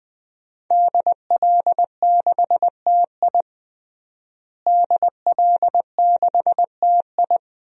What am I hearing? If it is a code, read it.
Morse code, heard as a keyed tone, about 20 wpm, 700 Hz. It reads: DL6TI DL6TI
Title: DXCC-100 Call Signs - 5 Characters - Repeated 2x - 20wpm